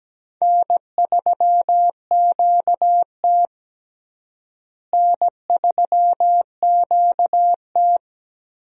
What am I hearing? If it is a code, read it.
Morse code, heard as a keyed tone, about 17 wpm, 700 Hz. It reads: N3QT N3QT
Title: USA Call Signs - 4 Characters - Repeated 3x - 17wpm